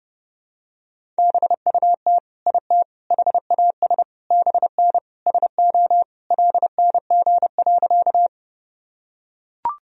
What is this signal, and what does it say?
Morse code, heard as a keyed tone, about 30 wpm, 700 Hz. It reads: BUT IT HAS BN SO LNG.
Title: Sentences from Top 100 Words and Abbreviations 30wpm